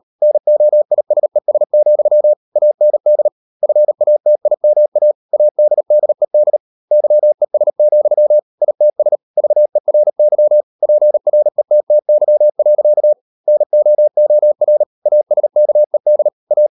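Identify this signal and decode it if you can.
Morse code, heard as a keyed tone, about 38 wpm, 600 Hz. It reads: NOISES, AND FATIMA ADDED YES, ITS VERY PRETTY. NOOR ASKED A
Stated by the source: Morse code practice at 38 WPM, sending Ken Follett's novel Never in English, from CW med Gnister.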